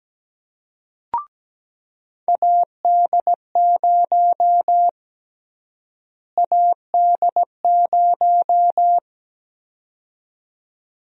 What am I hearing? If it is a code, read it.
Morse code, heard as a keyed tone, about 17 wpm, 700 Hz. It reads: AD0 AD0
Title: USA Call Sign Prefixes - Repeated 2x - 17wpm